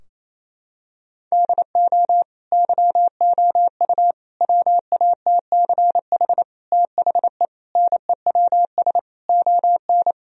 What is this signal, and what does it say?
Morse code, heard as a keyed tone, about 28 wpm, 700 Hz. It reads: DO YOU WATCH THE NEWS ON
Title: Sentences from Top 1000 Words - Review Part 4 - 28wpm